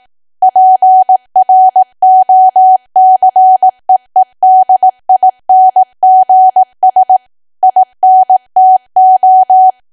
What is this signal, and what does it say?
Morse code, heard as a keyed tone, about 18 wpm, 750 Hz. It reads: PROCEEDINGS INTO
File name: cwa / W1AW-Level2 / 18WPM